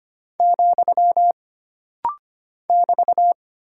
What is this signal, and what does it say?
Morse code, heard as a keyed tone, about 25 wpm, 700 Hz. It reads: , <BT>
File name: Single Punctuation-Prosign - Rapid-Fire - 25wpm